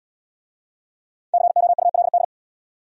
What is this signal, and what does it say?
Morse code, heard as a keyed tone, about 72 wpm, 700 Hz. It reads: COULD